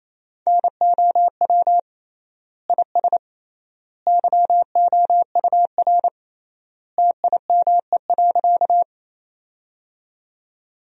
Morse code, heard as a keyed tone, about 27 wpm, 700 Hz. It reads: NOW IS YOUR TIME.
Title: 3x Word-Spacing - Sentences from Top 100 Words - 28wpm